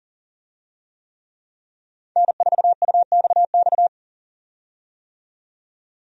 Morse code, heard as a keyed tone, about 38 wpm, 700 Hz. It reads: N4UXX
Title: USA Call Signs - 5 Characters - 40wpm